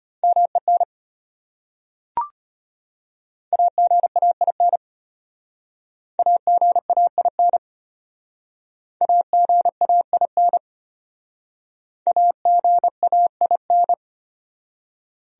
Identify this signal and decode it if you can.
Morse code, heard as a keyed tone, about 33 wpm, 700 Hz. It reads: MEN AGAIN AGAIN AGAIN AGAIN